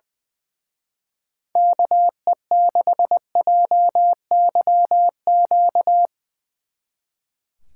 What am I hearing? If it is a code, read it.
Morse code, heard as a keyed tone, about 22 wpm, 700 Hz. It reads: KE6JYQ
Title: Speed Racing - USA Call Signs 20wpm